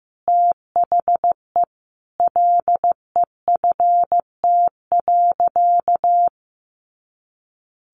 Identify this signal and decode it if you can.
Morse code, heard as a keyed tone, about 15 wpm, 700 Hz. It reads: THE LEFT.
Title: Sentences from Top 300 Words 15wpm